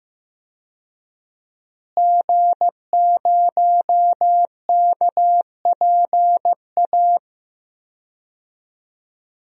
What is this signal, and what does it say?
Morse code, heard as a keyed tone, about 15 wpm, 700 Hz. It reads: G0KPA
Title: DXCC-100 Call Signs - 5 Characters - Repeated 2x - 15wpm